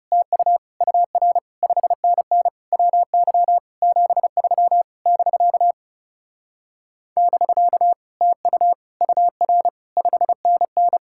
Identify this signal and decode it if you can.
Morse code, heard as a keyed tone, about 37 wpm, 700 Hz. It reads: TU UR 5NN WY 73 <BK> <BK> TU UR 5NN
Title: Speed-Racing Common QSO Phrases - 30wpm